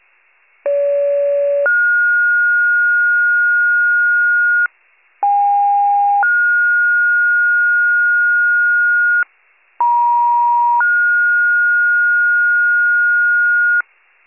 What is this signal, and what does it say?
Two-tone page, heard as a one-second tone, then a three-second tone, about 570 then 1430 Hz, about 810 then 1430 Hz, about 950 then 1430 Hz. It is heard from 0.7 to 13.8 s.
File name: 83 FIRE